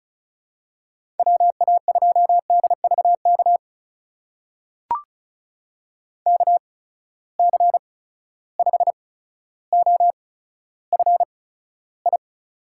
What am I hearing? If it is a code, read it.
Morse code, heard as a keyed tone, about 36 wpm, spaced as for 15 wpm, 700 Hz. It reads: WA2DVK KC5OFI
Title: USA Call Signs - 6 Characters - 8x Character Spacing - 35wpm